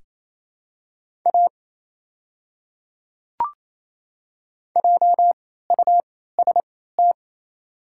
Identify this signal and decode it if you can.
Morse code, heard as a keyed tone, about 28 wpm, spaced as for 16 wpm, 700 Hz. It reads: A JUST
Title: 3x Character Spacing - Top 300 Words - 28wpm